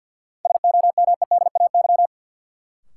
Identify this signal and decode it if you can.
Morse code, heard as a keyed tone, about 51 wpm, 700 Hz. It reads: SOMEDAY
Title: Sentences from Top 1600 Words - New Word Drill Alt - 50wpm